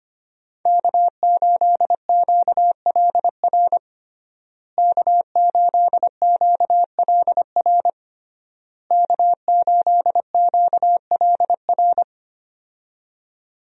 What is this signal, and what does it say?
Morse code, heard as a keyed tone, about 25 wpm, 700 Hz. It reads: K8QLR K8QLR K8QLR
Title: USA Call Signs - 5 Characters - Repeated 3x - 25wpm